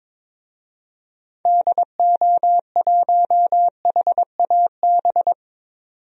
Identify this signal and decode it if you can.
Morse code, heard as a keyed tone, about 21 wpm, 700 Hz. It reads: DO1HAB
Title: DXCC-100 Call Signs - 6 Characters - 22wpm